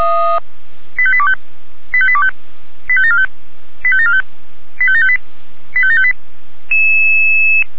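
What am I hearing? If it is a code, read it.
Five-tone selective call (ZVEI), heard as quick runs of five tones, tones of about 70 ms. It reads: 85825 85825 85745 85745 85758 85758
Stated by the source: ZVEI five-tone selective call received by SDR.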